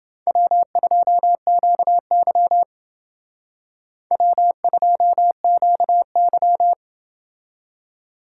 Morse code, heard as a keyed tone, about 29 wpm, 700 Hz. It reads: W2QY W2QY